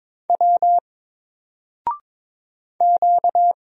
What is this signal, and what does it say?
Morse code, heard as a keyed tone, about 22 wpm, 700 Hz. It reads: W Q
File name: Single-Letter-Number-22wpm